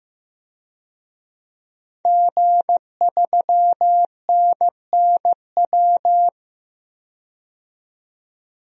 Morse code, heard as a keyed tone, about 16 wpm, 700 Hz. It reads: G3NNW
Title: DXCC-100 Call Signs - 5 Characters - 15wpm